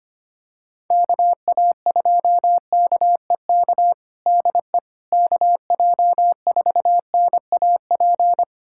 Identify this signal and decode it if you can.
Morse code, heard as a keyed tone, about 25 wpm, 700 Hz. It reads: KA2KEK DE KJ4NAP
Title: USA Callsign DE Callsign - 6 Characters - 25wpm